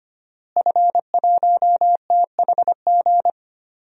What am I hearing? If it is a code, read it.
Morse code, heard as a keyed tone, about 25 wpm, 700 Hz. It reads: F1THG
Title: DXCC-100 Call Signs - 5 Characters - Repeated 3x - 25wpm